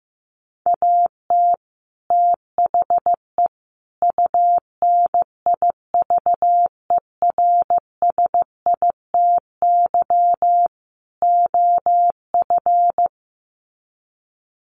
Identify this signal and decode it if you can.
Morse code, heard as a keyed tone, about 14 wpm, 700 Hz. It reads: AT THE UNIVERSITY OF
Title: Sets of 4 Words 15wpm